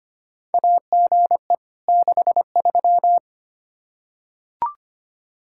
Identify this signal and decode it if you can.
Morse code, heard as a keyed tone, about 25 wpm, 700 Hz. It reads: AGE 63
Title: Practice QSO - Exchange 3 - More Info - 25wpm